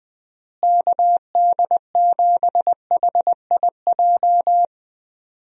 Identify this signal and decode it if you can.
Morse code, heard as a keyed tone, about 20 wpm, 700 Hz. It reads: KD7HIJ
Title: USA Call Signs - 6 Characters - Repeated 2x - 20wpm